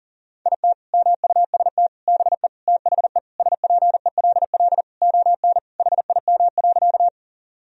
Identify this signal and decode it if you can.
Morse code, heard as a keyed tone, about 40 wpm, 700 Hz. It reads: IT MUST BE THE SPELL ON HIM.
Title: Sentences from Top 1000 Words - Review Part 1 - 40wpm